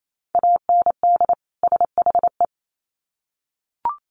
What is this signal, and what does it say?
Morse code, heard as a keyed tone, about 28 wpm, 700 Hz. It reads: AND SHE
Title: Sets of 2 Words 28wpm